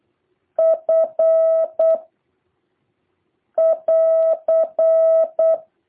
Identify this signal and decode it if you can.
Morse code, heard as a keyed tone, about 8 wpm, 650 Hz. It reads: F <AR>
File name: jh8cbh / 2023cw